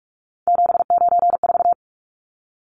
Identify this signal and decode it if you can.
Morse code, heard as a keyed tone, about 44 wpm, 700 Hz. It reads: 794
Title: Numbers - 3-Digit 45wpm